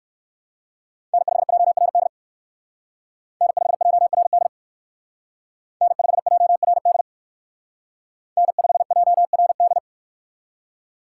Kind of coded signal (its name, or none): Morse code